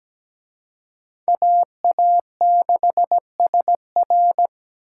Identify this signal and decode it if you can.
Morse code, heard as a keyed tone, about 20 wpm, 700 Hz. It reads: AA6SR
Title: Speed Racing - USA 2x2 Call Signs 17wpm